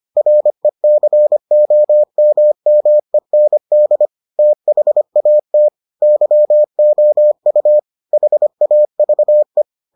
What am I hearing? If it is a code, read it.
Morse code, heard as a keyed tone, about 25 wpm, 600 Hz. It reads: RECOMMEND THAT YOU HAVE